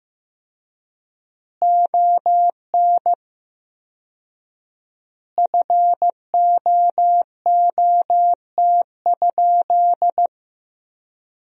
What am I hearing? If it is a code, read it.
Morse code, heard as a keyed tone, about 15 wpm, 700 Hz. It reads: ON FOOT?
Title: 4x Word-Spacing - Sentences from Top 500 Words - 15wpm